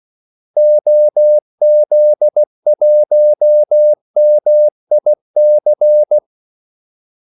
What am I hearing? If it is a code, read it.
Morse code, heard as a keyed tone, about 16 wpm, 600 Hz. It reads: OZ1MIC
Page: Kaldesignaler 16 wpm